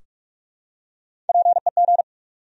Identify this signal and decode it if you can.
Morse code, heard as a keyed tone, about 44 wpm, 700 Hz. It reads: PEG
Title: USA Common Names - Max 4 Letters - 45wpm